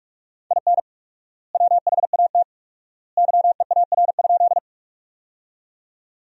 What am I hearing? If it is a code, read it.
Morse code, heard as a keyed tone, about 44 wpm, 700 Hz. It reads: IN WHAT YEAR?
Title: 4x Word-Spacing - Sentences from Top 200 Words - 45wpm